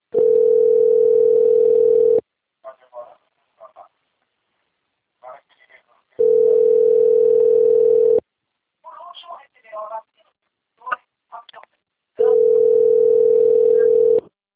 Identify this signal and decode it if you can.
Ringback tone (North America), 440 + 480 Hz, with speech.
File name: rg-1600-unknown- - - .35453